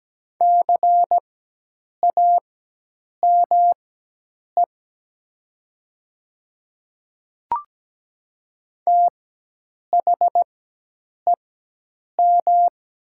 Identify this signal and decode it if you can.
Morse code, heard as a keyed tone, about 17 wpm, spaced as for 8 wpm, 700 Hz. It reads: CAME THEM
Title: 4x Character Spacing - Top 200 Words - 17wpm